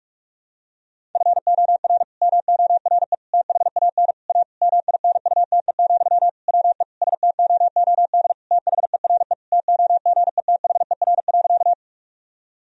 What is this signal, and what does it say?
Morse code, heard as a keyed tone, about 46 wpm, 700 Hz. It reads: FOR MORE THAN A MINUTE, WE STOOD THERE TOGETHER.
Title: Sentences from Top 1000 Words - Review Part 2 - 45wpm